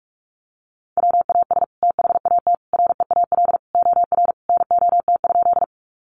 Morse code, heard as a keyed tone, about 44 wpm, 700 Hz. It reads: WAS THAT REAL OR NOT?
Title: Sentences from Top 600 Words 45wpm